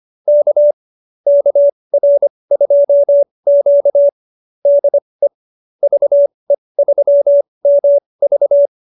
Morse code, heard as a keyed tone, about 25 wpm, spaced as for 20 wpm, 590 Hz. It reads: K KR2Q DE VE3MV